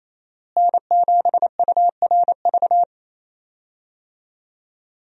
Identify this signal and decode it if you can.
Morse code, heard as a keyed tone, about 28 wpm, 700 Hz. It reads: N7URV
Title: USA Call Signs - 5 Characters - Repeated 3x - 28wpm